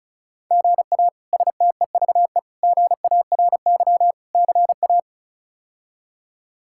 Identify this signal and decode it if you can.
Morse code, heard as a keyed tone, about 36 wpm, 700 Hz. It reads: GA STEVE GARY CA
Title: K1USN SST QSO Practice - 35wpm